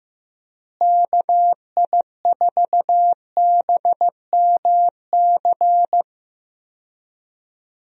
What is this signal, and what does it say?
Morse code, heard as a keyed tone, about 16 wpm, 700 Hz. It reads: KI4BMC
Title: USA Call Signs - 6 Characters - Repeated 3x - 15wpm